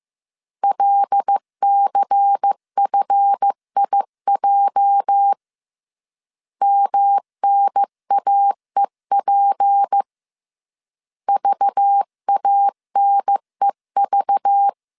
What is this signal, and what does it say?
Morse code, heard as a keyed tone, about 15 wpm, 800 Hz. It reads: LCFIJ MNAEP VANEV
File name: CW / DET / 1 9 1500C S B 14wpm 3302